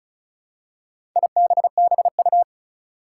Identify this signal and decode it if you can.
Morse code, heard as a keyed tone, about 36 wpm, 700 Hz. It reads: IBBU
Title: Four Challenging Letter-Number Combinations 35wpm